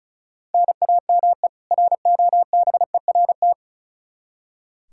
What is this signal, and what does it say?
Morse code, heard as a keyed tone, about 34 wpm, 700 Hz. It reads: NAME ROBERT
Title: Practice QSO - Exchange 1 - Beginning - 35wpm